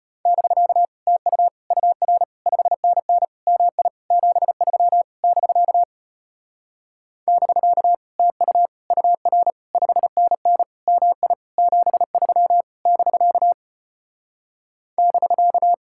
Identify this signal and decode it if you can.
Morse code, heard as a keyed tone, about 33 wpm, 700 Hz. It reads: <BK> TU UR 5NN MI 73 <BK> <BK> TU UR 5NN MI 73 <BK> <BK>